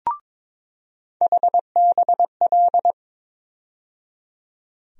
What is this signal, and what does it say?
Morse code, heard as a keyed tone, about 21 wpm, 700 Hz. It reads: HBL
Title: USA Call Sign Postfixes - 3 Letters 22wpm